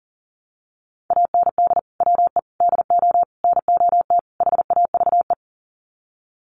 Morse code, heard as a keyed tone, about 40 wpm, 700 Hz. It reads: AND WE DO NOT HAVE
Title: Sets of 5 Words Encore 40wpm